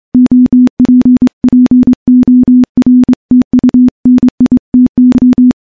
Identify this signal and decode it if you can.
Morse code, heard as a keyed tone, about 24 wpm, 250 Hz. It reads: OPPORTUNITY
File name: morse-training-25WPM